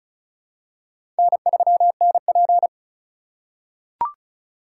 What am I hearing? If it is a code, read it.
Morse code, heard as a keyed tone, about 36 wpm, 700 Hz. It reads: N3NP
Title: USA Call Signs - 4 Characters - 6x Character Spacing - 35wpm